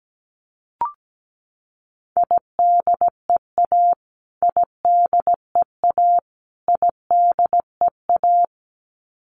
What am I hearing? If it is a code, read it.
Morse code, heard as a keyed tone, about 17 wpm, 700 Hz. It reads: IDEA IDEA IDEA
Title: Top 300 Words Repeated 17wpm